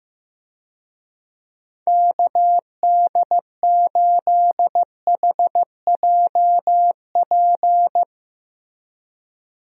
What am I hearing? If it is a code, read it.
Morse code, heard as a keyed tone, about 16 wpm, 700 Hz. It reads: KD8HJP
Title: USA Call Signs - 6 Characters - Repeated 2x - 15wpm